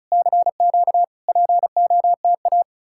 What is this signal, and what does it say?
Morse code, heard as a keyed tone, about 47 wpm, spaced as for 41 wpm, 700 Hz. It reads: CQ POTA
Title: Speed-Racing Common QSO Phrases - 35wpm